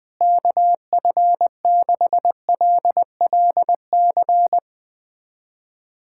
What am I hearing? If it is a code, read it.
Morse code, heard as a keyed tone, about 20 wpm, 700 Hz. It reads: KF6LLC
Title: USA Callsign DE Callsign - 6 Characters - 20wpm